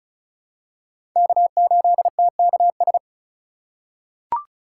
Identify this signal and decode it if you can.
Morse code, heard as a keyed tone, about 34 wpm, 700 Hz. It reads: K8TKS
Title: USA Call Signs - 5 Characters - 2x Character Spacing - 35wpm